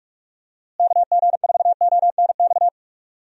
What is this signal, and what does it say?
Morse code, heard as a keyed tone, about 46 wpm, 700 Hz. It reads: KG4ONX
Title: USA Call Signs - 6 Characters - Repeated 2x - 45wpm